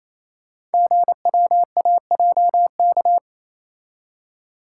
Morse code, heard as a keyed tone, about 29 wpm, 700 Hz. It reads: GWAJK
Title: Code Groups - 5 Letter:Numbers - 28wpm